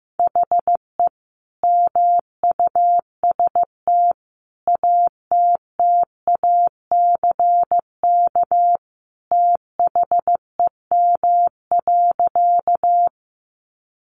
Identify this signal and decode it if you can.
Morse code, heard as a keyed tone, about 15 wpm, 700 Hz. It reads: HE MUST ATTACK THEM.
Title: Sentences from Top 1500 Words 15wpm